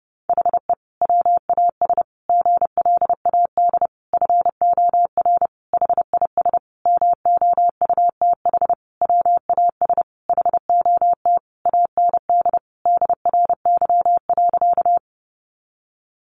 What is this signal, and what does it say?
Morse code, heard as a keyed tone, about 30 wpm, 700 Hz. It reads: HE WAS GLAD FOR HIS MOUTH WAS HOT AND DRY.
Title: Sentences from Top 1500 Words 30wpm